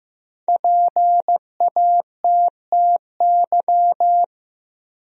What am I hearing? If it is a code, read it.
Morse code, heard as a keyed tone, about 15 wpm, 700 Hz. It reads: PATTY
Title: USA Common Names - Max 5 Letters - 15wpm